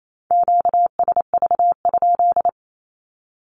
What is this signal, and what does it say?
Morse code, heard as a keyed tone, about 28 wpm, 700 Hz. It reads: QSV?
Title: Q-Codes Repeated 3x 28wpm